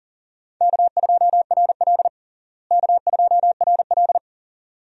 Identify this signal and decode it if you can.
Morse code, heard as a keyed tone, about 40 wpm, 700 Hz. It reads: K2RL K2RL
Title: USA Call Signs - 4 Characters - Repeated 2x - 40wpm